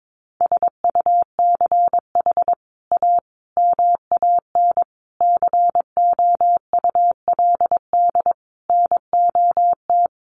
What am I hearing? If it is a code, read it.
Morse code, heard as a keyed tone, about 22 wpm, 700 Hz. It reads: SUCH A MAN COULD NOT
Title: Sentences from Top 1100 Words 22wpm